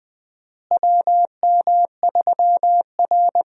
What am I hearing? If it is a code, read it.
Morse code, heard as a keyed tone, about 20 wpm, 700 Hz. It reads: WM3R
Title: USA Call Signs - 4 Characters - Repeated 3x - 20wpm